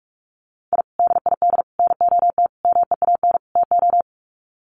Morse code, heard as a keyed tone, about 44 wpm, 700 Hz. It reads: I DID NOT MEAN TO
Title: Sets of 5 Words 45wpm